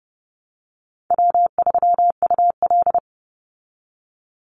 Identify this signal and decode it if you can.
Morse code, heard as a keyed tone, about 30 wpm, 700 Hz. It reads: W3UL
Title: Easy Call Signs Repeated 3x - 30wpm